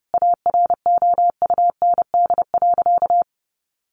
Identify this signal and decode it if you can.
Morse code, heard as a keyed tone, about 30 wpm, 700 Hz. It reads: AROUND.
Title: Easy Phrases 30wpm